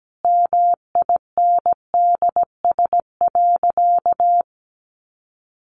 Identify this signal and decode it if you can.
Morse code, heard as a keyed tone, about 17 wpm, 700 Hz. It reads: MINDS.
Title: Easy Phrases 17wpm